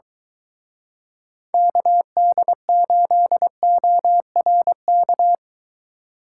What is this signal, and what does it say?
Morse code, heard as a keyed tone, about 22 wpm, 700 Hz. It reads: KD8ORK